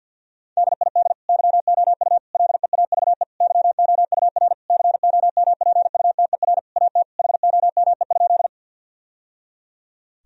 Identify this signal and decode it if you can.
Morse code, heard as a keyed tone, about 50 wpm, 700 Hz. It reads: DID YOU LEAVE YOUR COMPUTER AT HOME?
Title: Sentences from Top 1000 Words - Review Part 3 - 50wpm